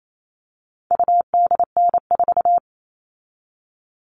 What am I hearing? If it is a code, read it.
Morse code, heard as a keyed tone, about 28 wpm, 700 Hz. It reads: UDN4
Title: All Cut Numbers - 4-Digit 28wpm